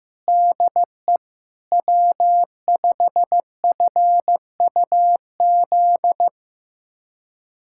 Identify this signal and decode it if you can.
Morse code, heard as a keyed tone, about 15 wpm, 700 Hz. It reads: DE W5FUZ
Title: USA Callsign DE Callsign - 5 Characters - 15wpm